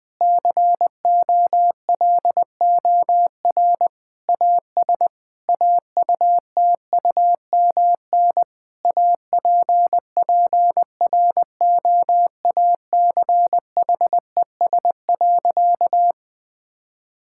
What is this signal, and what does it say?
Morse code, heard as a keyed tone, about 20 wpm, 700 Hz. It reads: COLOR AS AUTUMN APPROACHES.
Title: Difficult Sentences with Hints - 20wpm